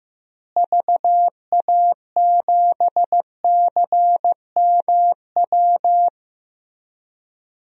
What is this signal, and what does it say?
Morse code, heard as a keyed tone, about 15 wpm, 700 Hz. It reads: VA7CMW
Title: DXCC-100 Call Signs - 6 Characters - Repeated 3x - 15wpm